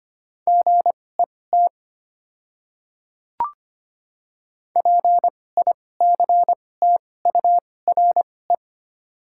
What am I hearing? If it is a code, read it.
Morse code, heard as a keyed tone, about 25 wpm, spaced as for 18 wpm, 690 Hz. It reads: GET PICTURE
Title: 2x Character Spacing - Top 200 Words - 25wpm